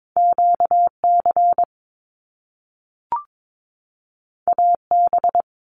Morse code, heard as a keyed tone, about 22 wpm, 700 Hz. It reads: QC AB
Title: Canadian Provinces and Territories 22wpm